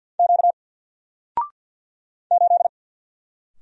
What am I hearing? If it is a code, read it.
Morse code, heard as a keyed tone, about 48 wpm, 700 Hz. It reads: Q 8
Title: Single Character (Letter:Number:Punctuation:Prosign) 50wpm